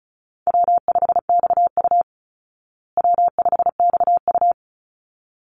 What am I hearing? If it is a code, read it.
Morse code, heard as a keyed tone, about 34 wpm, 700 Hz. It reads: W5XU W5XU
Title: Easy Call Signs Repeated 2x - 35wpm